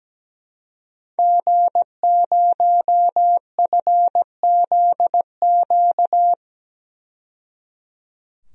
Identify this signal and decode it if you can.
Morse code, heard as a keyed tone, about 17 wpm, 700 Hz. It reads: G0FZQ
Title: DXCC-100 Call Signs - 5 Characters - 17wpm